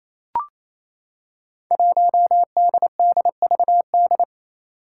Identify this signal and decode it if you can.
Morse code, heard as a keyed tone, about 27 wpm, 700 Hz. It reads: 1DDVD
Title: Five Challenging Letter-Number Combinations 28wpm